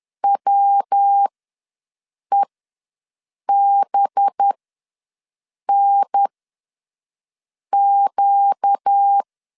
Morse code, heard as a keyed tone, about 10 wpm, spaced as for 5 wpm, 800 Hz. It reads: WEBNQ
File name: LEZIONI / 6 PKX0 / 6 PK ws 10wpm 3745